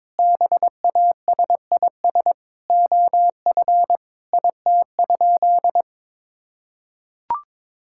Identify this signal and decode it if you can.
Morse code, heard as a keyed tone, about 22 wpm, 700 Hz. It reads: BASIS OF IT?
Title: Sentences from Top 1000 Words - Review Part 5 - 22wpm